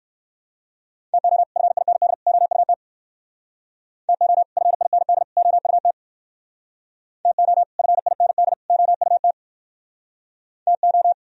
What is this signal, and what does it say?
Morse code, heard as a keyed tone, about 59 wpm, 700 Hz. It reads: TO FIND OUT TO FIND OUT TO FIND OUT TO